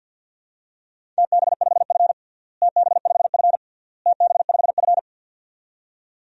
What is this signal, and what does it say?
Morse code, heard as a keyed tone, about 50 wpm, 700 Hz. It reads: TB5F TB5F TB5F
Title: DXCC-100 Call Signs - 4 Characters - Repeated 3x - 50wpm